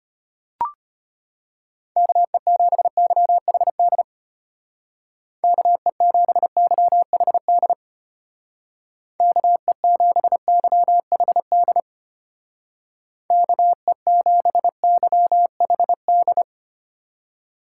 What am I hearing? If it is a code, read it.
Morse code, heard as a keyed tone, about 32 wpm, 700 Hz. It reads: KE7YHD KE7YHD KE7YHD KE7YHD